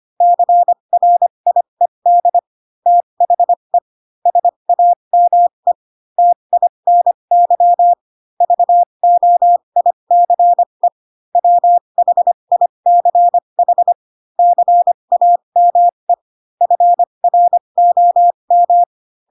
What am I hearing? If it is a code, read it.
Morse code, heard as a keyed tone, about 25 wpm, spaced as for 22 wpm, 700 Hz. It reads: CRIED THE SAME TINY VOICE WHICH CAME FROM
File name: Pinocchio / Pinocchio ch2 18 25 ik1hge